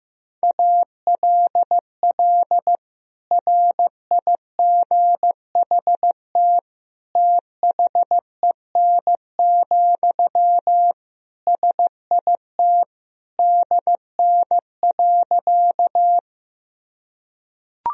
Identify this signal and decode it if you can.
Morse code, heard as a keyed tone, about 15 wpm, 700 Hz. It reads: ALL RIGHT THEN, SIT DN.
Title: Sentences from Top 500 Words and Abbreviations 15wpm